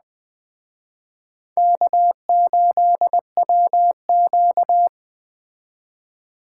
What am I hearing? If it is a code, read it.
Morse code, heard as a keyed tone, about 21 wpm, 700 Hz. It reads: K8WQ